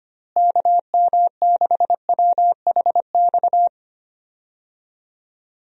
Morse code, heard as a keyed tone, about 25 wpm, 700 Hz. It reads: KM6WHX
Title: USA Call Signs - 6 Characters - Repeated 3x - 25wpm